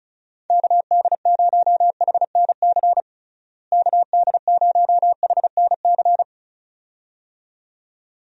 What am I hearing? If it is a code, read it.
Morse code, heard as a keyed tone, about 36 wpm, 700 Hz. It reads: KD0HNC KD0HNC
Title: USA Call Signs - 6 Characters - Repeated 2x - 35wpm